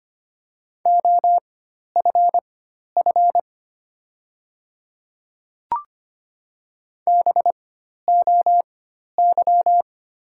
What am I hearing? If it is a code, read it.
Morse code, heard as a keyed tone, about 25 wpm, spaced as for 12 wpm, 700 Hz. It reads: OFF BOY
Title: 4x Character Spacing - Top 200 Words - 25wpm